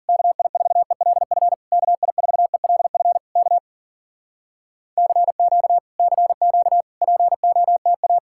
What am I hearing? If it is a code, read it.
Morse code, heard as a keyed tone, about 52 wpm, 700 Hz. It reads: KI4ELF KI4ELF K CQ CQ POTA